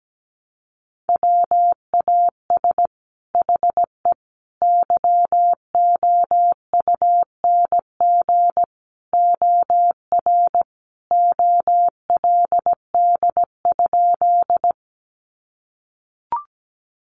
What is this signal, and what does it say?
Morse code, heard as a keyed tone, about 17 wpm, 700 Hz. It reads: WAS HE YOUNG OR OLD?
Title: Sentences from Top 400 Words 17wpm